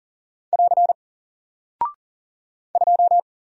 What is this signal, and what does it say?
Morse code, heard as a keyed tone, about 40 wpm, 700 Hz. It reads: <AR> 2
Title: Single Character (Letter:Number:Punctuation:Prosign) Alt 40wpm